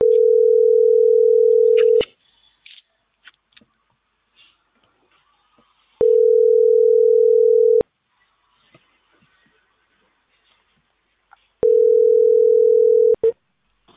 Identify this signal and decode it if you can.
Ringback tone (North America), 440 + 480 Hz.